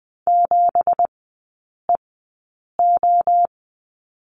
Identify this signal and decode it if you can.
Morse code, heard as a keyed tone, about 20 wpm, spaced as for 8 wpm, 700 Hz. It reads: 7EO
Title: 198 Lesson 32e - Letter k - Three Characters 20wpm